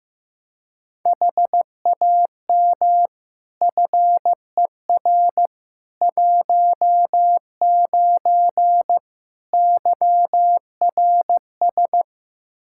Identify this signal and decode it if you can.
Morse code, heard as a keyed tone, about 15 wpm, 700 Hz. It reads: HAM FER 19 YRS
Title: Practice QSO - Exchange 3 - More Info - 15wpm